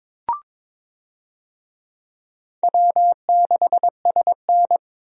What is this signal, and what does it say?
Morse code, heard as a keyed tone, about 23 wpm, 700 Hz. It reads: W6SN
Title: USA Call Signs - 4 Characters - 22wpm